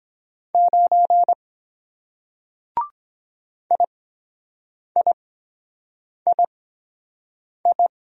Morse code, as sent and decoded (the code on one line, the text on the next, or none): ----. / .. .. .. ..
9 IIII